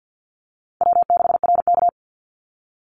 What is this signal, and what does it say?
Morse code, heard as a keyed tone, about 50 wpm, 700 Hz. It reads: W6RK
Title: Easy Call Signs Repeated 2x - 50wpm